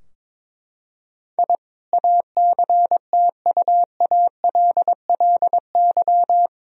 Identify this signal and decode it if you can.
Morse code, heard as a keyed tone, about 22 wpm, 700 Hz. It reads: I ACTUALLY
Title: Sentences from Top 1000 Words - Review Part 4 - 22wpm